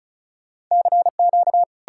Morse code, heard as a keyed tone, about 48 wpm, 700 Hz. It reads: CQ